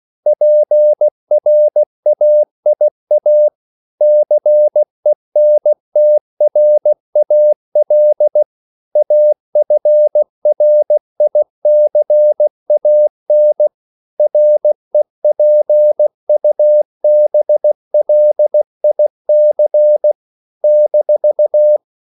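Morse code, heard as a keyed tone, about 16 wpm, 600 Hz. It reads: PRAIA CENTRAL AFRICAN REPUBLIC -
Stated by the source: CW practice of national capitals at 16 words per minute, from ON6ZQ.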